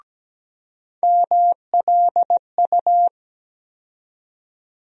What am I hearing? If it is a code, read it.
Morse code, heard as a keyed tone, about 17 wpm, 700 Hz. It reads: MLU
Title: USA Call Sign Postfixes - 3 Letters 17wpm